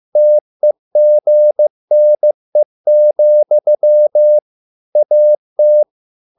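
Morse code, heard as a keyed tone, about 15 wpm, 600 Hz. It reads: TEGNE, AT
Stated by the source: Morse code at a slow 15 wpm from CW med Gnister.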